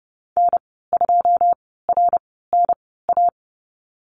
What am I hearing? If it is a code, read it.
Morse code, heard as a keyed tone, about 30 wpm, spaced as for 17 wpm, 700 Hz. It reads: N2RNA
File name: Lesson 14g - Number 2 - Call Signs 30wpm